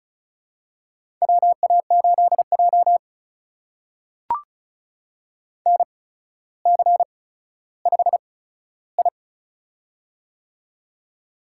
Morse code, heard as a keyed tone, about 36 wpm, spaced as for 15 wpm, 700 Hz. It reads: WA8J NC5I
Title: USA Call Signs - 4 Characters - 8x Character Spacing - 35wpm